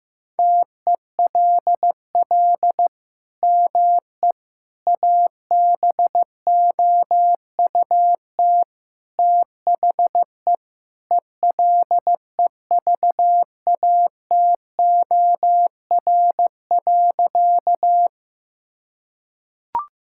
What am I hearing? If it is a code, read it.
Morse code, heard as a keyed tone, about 15 wpm, 700 Hz. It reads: TELL ME ABOUT THE ELEVATOR.
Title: Sentences from Top 1600 Words 15wpm